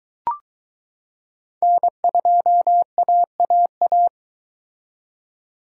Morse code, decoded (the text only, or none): N2AAA